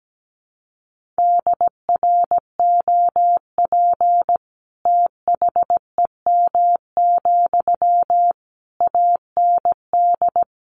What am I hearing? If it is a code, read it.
Morse code, heard as a keyed tone, about 17 wpm, 700 Hz. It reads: DROP THEM, AND
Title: Sentences from Top 1400 Words 17wpm